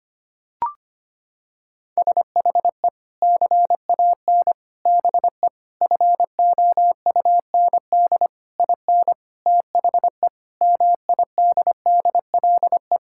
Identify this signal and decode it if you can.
Morse code, heard as a keyed tone, about 25 wpm, 700 Hz. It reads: SHE CAN BE FOUND IN THE MIDDLE
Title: Sentences from Top 1000 Words - Review Part 6 - 25wpm